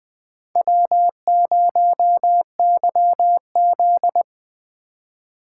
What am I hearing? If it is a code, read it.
Morse code, heard as a keyed tone, about 21 wpm, 700 Hz. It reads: W0YZ
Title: Speed Racing - USA 1x2 Call Signs 17wpm